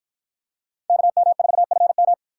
Morse code, heard as a keyed tone, about 53 wpm, 700 Hz. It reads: KM4FM